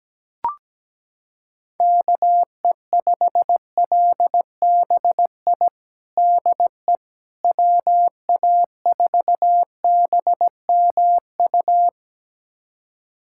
Morse code, heard as a keyed tone, about 17 wpm, 700 Hz. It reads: KE5LBI DE WA4BMU
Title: USA Callsign DE Callsign - 6 Characters - 17wpm